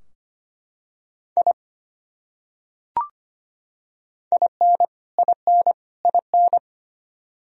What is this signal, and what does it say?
Morse code, heard as a keyed tone, about 25 wpm, 700 Hz. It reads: I IN IN IN
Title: Top 100 Words Repeated 25wpm